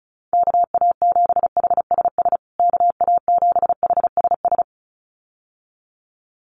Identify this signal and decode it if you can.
Morse code, heard as a keyed tone, about 34 wpm, 700 Hz. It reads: KA7HSS KA7HSS
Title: USA Call Signs Repeated 2x 35wpm